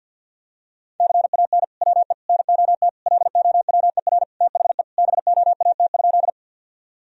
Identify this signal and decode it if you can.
Morse code, heard as a keyed tone, about 52 wpm, 700 Hz. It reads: CAN WE NOT LOWER THE BOAT?
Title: Sentences from Top 1000 Words - Review Part 4 - 50wpm